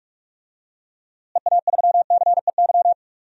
Morse code, heard as a keyed tone, about 41 wpm, 700 Hz. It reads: EA3CEY
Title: DXCC-100 Call Signs - 6 Characters - Repeated 2x - 45wpm